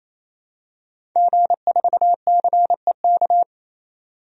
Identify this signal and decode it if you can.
Morse code, heard as a keyed tone, about 28 wpm, 700 Hz. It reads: G4CEK
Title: DXCC-100 Call Signs - 5 Characters - 28wpm